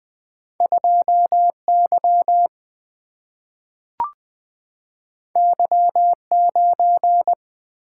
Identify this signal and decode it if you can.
Morse code, heard as a keyed tone, about 20 wpm, 700 Hz. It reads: 2Y Y9
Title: Single-Letter Single-Number 20wpm